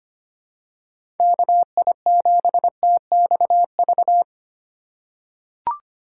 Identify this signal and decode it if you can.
Morse code, heard as a keyed tone, about 25 wpm, 700 Hz. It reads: KI7TXV
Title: USA Call Signs - 6 Characters - 4x Character Spacing - 25wpm